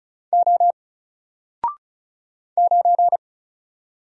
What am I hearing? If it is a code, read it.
Morse code, heard as a keyed tone, about 36 wpm, 700 Hz. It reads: O 9
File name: Single-Letter-Number-35wpm